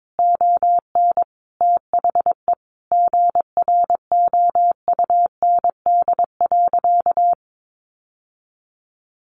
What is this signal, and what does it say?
Morse code, heard as a keyed tone, about 22 wpm, 700 Hz. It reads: ON THE GROUND.
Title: Sentences from Top 1500 Words 22wpm